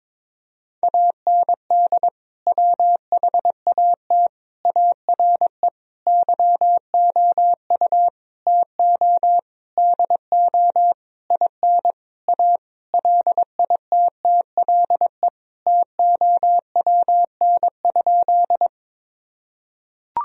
Morse code, heard as a keyed tone, about 22 wpm, 700 Hz. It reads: AND WHAT ARE YOU TO DO IN A LITTLE TOWN?
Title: Sentences from Top 1000 Words - Review Part 2 - 22wpm